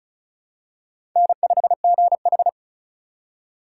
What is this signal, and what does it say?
Morse code, heard as a keyed tone, about 36 wpm, 700 Hz. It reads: N5GH
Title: USA Call Signs - 4 Characters - 2x Character Spacing - 35wpm